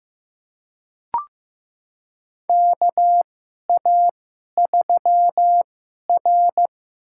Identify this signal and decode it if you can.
Morse code, heard as a keyed tone, about 15 wpm, spaced as for 11 wpm, 700 Hz. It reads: KA3R
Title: USA Call Signs - 2x Character Spacing - 15wpm